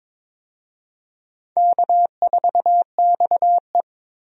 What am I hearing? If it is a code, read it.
Morse code, heard as a keyed tone, about 21 wpm, 700 Hz. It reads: K4XE
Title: USA Call Signs - 4 Characters - Repeated 2x - 22wpm